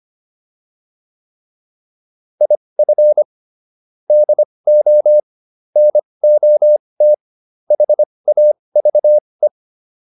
Morse code, heard as a keyed tone, about 25 wpm, spaced as for 20 wpm, 600 Hz. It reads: IF DO NOT HAVE